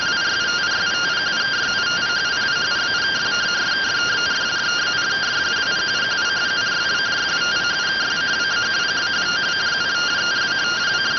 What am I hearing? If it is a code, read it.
Two-tone teleprinter signal, about 170 Hz shift.